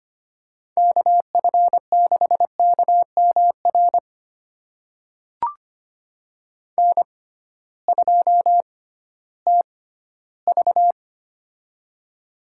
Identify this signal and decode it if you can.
Morse code, heard as a keyed tone, about 25 wpm, spaced as for 14 wpm, 700 Hz. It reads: KF6KMR N2TV
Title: USA Call Signs - 6x Character Spacing - 25wpm